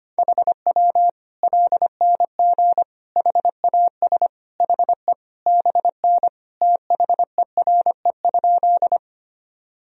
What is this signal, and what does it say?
Morse code, heard as a keyed tone, about 25 wpm, 700 Hz. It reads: HW LNG HAS HE BN THERE?
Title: Sentences from Top 100 Words and Abbreviations 25wpm